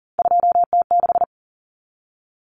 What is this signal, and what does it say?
Morse code, heard as a keyed tone, about 40 wpm, 700 Hz. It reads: 2T6
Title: Common Cut Numbers - 3-Digit 40wpm